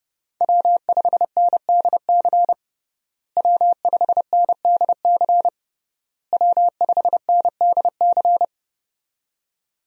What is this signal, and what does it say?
Morse code, heard as a keyed tone, about 32 wpm, 700 Hz. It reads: W5NDC W5NDC W5NDC
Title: USA Call Signs - 5 Characters - Repeated 3x - 30wpm